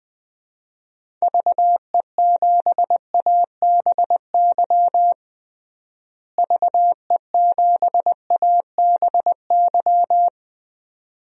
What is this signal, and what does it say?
Morse code, heard as a keyed tone, about 20 wpm, 700 Hz. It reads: VE7ABY VE7ABY
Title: DXCC-100 Call Signs - 6 Characters - Repeated 2x - 20wpm